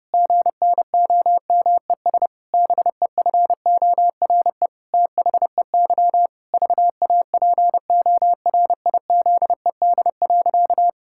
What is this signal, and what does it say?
Morse code, heard as a keyed tone, about 32 wpm, 700 Hz. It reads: GNOMES BEFORE THEY VAPORIZED.
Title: Pangram Challenge v2 - 30wpm